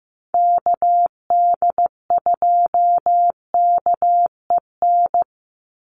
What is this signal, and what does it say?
Morse code, heard as a keyed tone, about 15 wpm, 700 Hz. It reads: KD2KEN
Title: USA Call Signs 15wpm